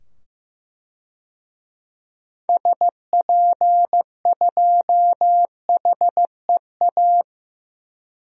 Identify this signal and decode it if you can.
Morse code, heard as a keyed tone, about 16 wpm, 700 Hz. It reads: SP2HEA
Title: DXCC-100 Call Signs - 6 Characters - 15wpm